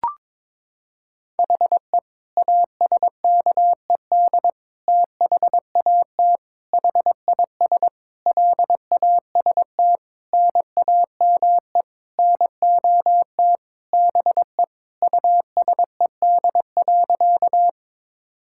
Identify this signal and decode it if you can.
Morse code, heard as a keyed tone, about 22 wpm, 700 Hz. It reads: HE ASKED THAT HIS LAST NAME NOT BE USED.
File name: Sentences with 9-Word Phrases - 22wpm